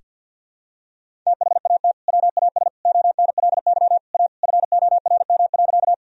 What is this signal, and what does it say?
Morse code, heard as a keyed tone, about 51 wpm, 700 Hz. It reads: THAT WAS ONLY A FORM.
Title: Sentences from Top 1000 Words - Review Part 1 - 50wpm